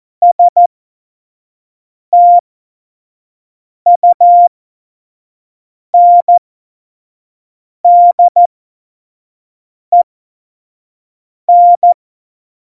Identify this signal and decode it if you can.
Morse code, heard as a keyed tone, about 14 wpm, spaced as for 5 wpm, 700 Hz. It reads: STUNDEN
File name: Morsekurs / Witze / Witz 15. Mai 2018, 25 F70